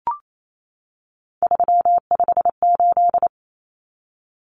Morse code, heard as a keyed tone, about 28 wpm, 700 Hz. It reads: 358
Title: Common Cut Numbers - 3-Digit 28wpm